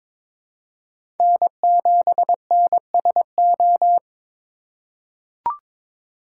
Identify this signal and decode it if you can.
Morse code, heard as a keyed tone, about 22 wpm, 700 Hz. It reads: N7NSO
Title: USA Call Signs - 5 Characters - 2x Character Spacing - 22wpm